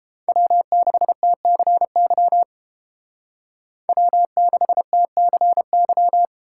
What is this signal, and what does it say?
Morse code, heard as a keyed tone, about 30 wpm, 700 Hz. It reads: W6TCY W6TCY